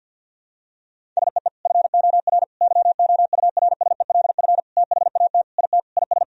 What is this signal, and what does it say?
Morse code, heard as a keyed tone, about 52 wpm, 700 Hz. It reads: SEE FOR YOURSELF THAT IT IS
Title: Sentences from Top 1000 Words - Review Part 1 - 50wpm